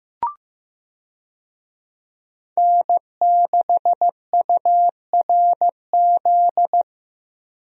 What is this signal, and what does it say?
Morse code, heard as a keyed tone, about 15 wpm, 700 Hz. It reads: N6URZ
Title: USA Call Signs - 5 Characters - Repeated 2x - 15wpm